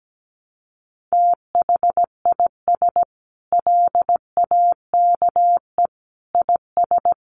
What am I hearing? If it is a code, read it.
Morse code, heard as a keyed tone, about 17 wpm, 700 Hz. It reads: THIS LAKE IS
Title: Sentences from Top 1100 Words 17wpm